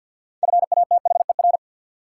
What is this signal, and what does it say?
Morse code, heard as a keyed tone, about 53 wpm, 700 Hz. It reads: FATHER
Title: Sentences from Top 300 Words - New Word Drill - 50wpm